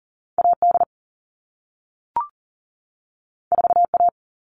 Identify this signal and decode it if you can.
Morse code, heard as a keyed tone, about 40 wpm, 700 Hz. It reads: AD 4A
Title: All Cut Numbers - 2-Digit 40wpm